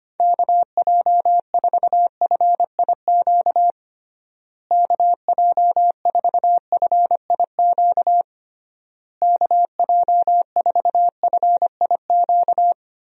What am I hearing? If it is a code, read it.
Morse code, heard as a keyed tone, about 25 wpm, 700 Hz. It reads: KJ4FIQ KJ4FIQ KJ4FIQ
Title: USA Call Signs - 6 Characters - Repeated 3x - 25wpm